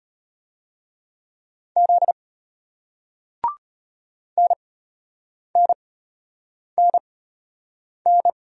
Morse code, heard as a keyed tone, about 34 wpm, spaced as for 8 wpm, 700 Hz. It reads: Z NNNN